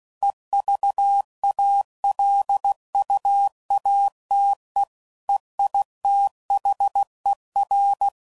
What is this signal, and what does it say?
Morse code, heard as a keyed tone, about 16 wpm, 800 Hz. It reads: EVALUATE EITHER
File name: CW Code Groups @15WPM / Technical Subjects / AND Gate